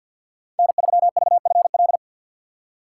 Morse code, heard as a keyed tone, about 50 wpm, 700 Hz. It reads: N3VFL
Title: USA Call Signs - 5 Characters - 4x Character Spacing - 50wpm